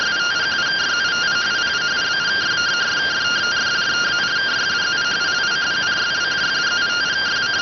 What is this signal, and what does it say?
Two-tone teleprinter signal, about 170 Hz shift.